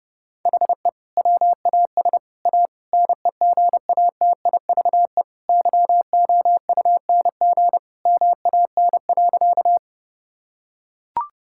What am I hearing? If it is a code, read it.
Morse code, heard as a keyed tone, about 32 wpm, 700 Hz. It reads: HE WAS A NEGATIVE YOUNG MAN.
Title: Sentences from Top 1000 Words - Review Part 6 - 30wpm